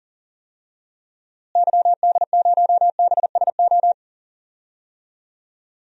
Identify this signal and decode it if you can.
Morse code, heard as a keyed tone, about 40 wpm, 700 Hz. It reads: YD0BSO
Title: DXCC-100 Call Signs - 6 Characters - Repeated 3x - 40wpm